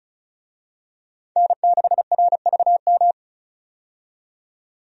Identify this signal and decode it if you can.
Morse code, heard as a keyed tone, about 36 wpm, 700 Hz. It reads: N6RVM
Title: USA Call Signs - 5 Characters - 35wpm